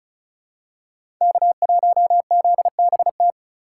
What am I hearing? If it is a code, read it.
Morse code, heard as a keyed tone, about 36 wpm, 700 Hz. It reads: K1ZBT
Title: USA Call Signs - 5 Characters - 3x Character Spacing - 35wpm